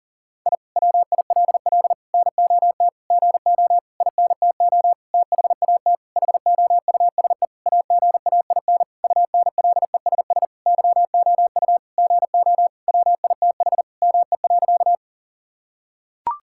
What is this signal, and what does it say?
Morse code, heard as a keyed tone, about 40 wpm, 700 Hz. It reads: I WILL NOT GO INTO THAT HOUSE AGAIN UNLESS YOU GO WITH ME.
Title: Sentences from Top 1000 Words - Review Part 6 - 40wpm